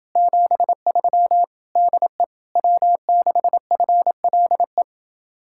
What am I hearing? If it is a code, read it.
Morse code, heard as a keyed tone, about 28 wpm, 700 Hz. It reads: 73 DE W6FLE